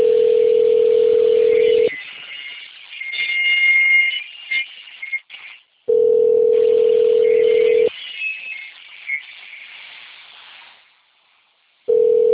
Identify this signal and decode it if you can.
Ringback tone (North America), 440 + 480 Hz.